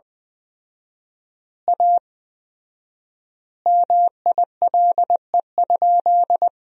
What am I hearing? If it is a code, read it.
Morse code, heard as a keyed tone, about 20 wpm, 700 Hz. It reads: A MILE?
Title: 4x Word-Spacing - Sentences from Top 400 Words - 20wpm